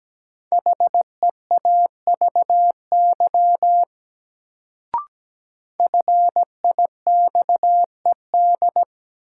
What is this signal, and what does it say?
Morse code, heard as a keyed tone, about 17 wpm, 700 Hz. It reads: HEAVY FIXED
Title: Sentences from Top 1100 Words - New Word Drill Alt - 17wpm